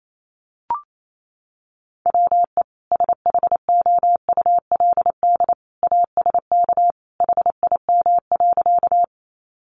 Morse code, heard as a keyed tone, about 28 wpm, 700 Hz. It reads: WE SHOULD ASK HIM.
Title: Sentences from Top 300 Words 28wpm